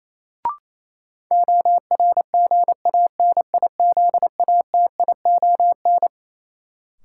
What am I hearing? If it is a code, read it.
Morse code, heard as a keyed tone, about 28 wpm, 700 Hz. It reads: ORGANIZATION
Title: Sentences from Top 1100 Words - New Word Drill Alt - 28wpm